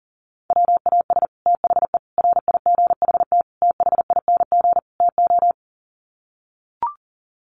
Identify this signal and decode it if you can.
Morse code, heard as a keyed tone, about 40 wpm, 700 Hz. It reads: WAS THE RIGHT THING TO
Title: Sets of 5 Words Encore2 40wpm